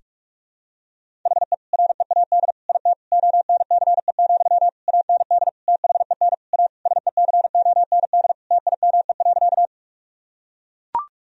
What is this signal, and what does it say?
Morse code, heard as a keyed tone, about 42 wpm, 700 Hz. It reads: HE READ IT ONCE, AND THEN A SECOND TIME.
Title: Sentences from Top 1000 Words - Review Part 1 - 45wpm